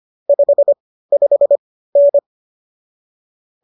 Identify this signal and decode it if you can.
Morse code, heard as a keyed tone, about 25 wpm, spaced as for 15 wpm, 570 Hz. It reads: 55N